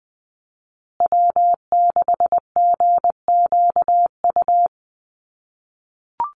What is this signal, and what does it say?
Morse code, heard as a keyed tone, about 20 wpm, 700 Hz. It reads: W6GQU
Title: USA Call Signs Repeated 2x 20wpm